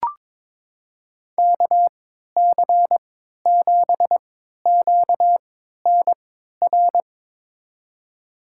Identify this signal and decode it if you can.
Morse code, heard as a keyed tone, about 22 wpm, spaced as for 12 wpm, 700 Hz. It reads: KC7QNR
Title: USA Call Signs - 6 Characters - 3x Character Spacing - 22wpm